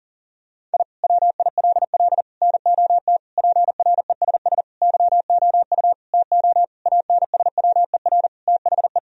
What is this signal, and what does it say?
Morse code, heard as a keyed tone, about 40 wpm, 700 Hz. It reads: I WILL NOT PRESS YOU TO ANSWER THE
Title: Sentences from Top 1000 Words - Review Part 1 - 40wpm